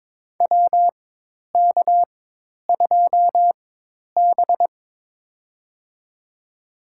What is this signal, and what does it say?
Morse code, heard as a keyed tone, about 22 wpm, spaced as for 10 wpm, 700 Hz. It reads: WK2B
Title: USA Call Signs - 4x Character Spacing - 22wpm